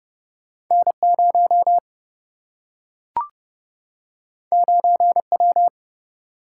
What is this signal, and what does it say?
Morse code, heard as a keyed tone, about 30 wpm, 700 Hz. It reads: N0 9W
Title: Single-Letter Single-Number 30wpm